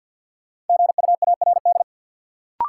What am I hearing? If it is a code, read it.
Morse code, heard as a keyed tone, about 51 wpm, 700 Hz. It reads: GUARD
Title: Sentences from Top 700 Words - New Word Drill - 50wpm